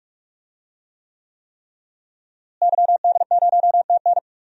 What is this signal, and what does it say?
Morse code, heard as a keyed tone, about 44 wpm, 700 Hz. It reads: YD0TN
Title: DXCC-100 Call Signs - 5 Characters - Repeated 3x - 45wpm